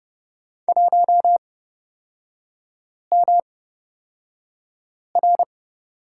Morse code, heard as a keyed tone, about 30 wpm, spaced as for 5 wpm, 700 Hz. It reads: 1MR
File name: Lesson 37e - Letter x - Three Characters 30wpm